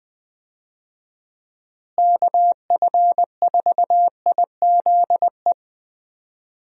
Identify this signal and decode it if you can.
Morse code, heard as a keyed tone, about 20 wpm, 700 Hz. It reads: KF4IZE
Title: USA Call Signs - 6 Characters - Repeated 2x - 20wpm